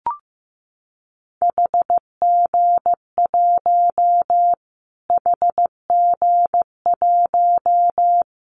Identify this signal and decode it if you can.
Morse code, heard as a keyed tone, about 15 wpm, 700 Hz. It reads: HG1 HG1
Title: Callsign Prefixes for Top 25 DXCC - Repeated 3x - 15wpm